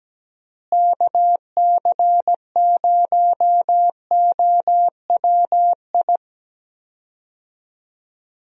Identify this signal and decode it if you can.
Morse code, heard as a keyed tone, about 17 wpm, 700 Hz. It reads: KC0OWI
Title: USA Call Signs - 6 Characters - Repeated 3x - 17wpm